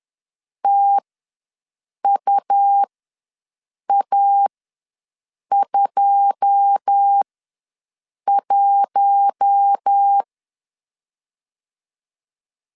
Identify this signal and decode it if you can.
Morse code, heard as a keyed tone, about 10 wpm, spaced as for 6 wpm, 800 Hz. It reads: TUA21
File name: LEZIONI / 9 2 ws 10wpm 4130